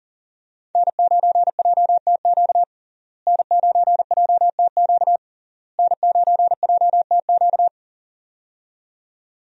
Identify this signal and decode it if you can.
Morse code, heard as a keyed tone, about 40 wpm, 700 Hz. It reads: N9JTQ N9JTQ N9JTQ
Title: USA Call Signs - 5 Characters - Repeated 3x - 40wpm